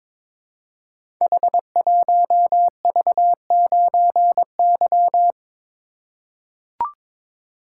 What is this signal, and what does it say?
Morse code, heard as a keyed tone, about 22 wpm, 700 Hz. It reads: H1V9Y
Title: Code Groups - 5 Letter:Numbers - 22wpm